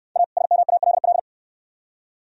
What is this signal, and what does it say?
Morse code, heard as a keyed tone, about 69 wpm, 690 Hz. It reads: A SMALL